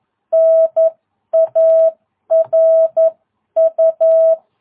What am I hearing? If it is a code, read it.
Morse code, heard as a keyed tone, about 11 wpm, 660 Hz. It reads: NARU
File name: jh8cbh / 2022cw